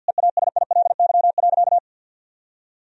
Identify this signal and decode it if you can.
Morse code, heard as a keyed tone, about 51 wpm, 700 Hz. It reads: EASILY.
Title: Sentences from Top 1000 Words - Review Part 1 - 50wpm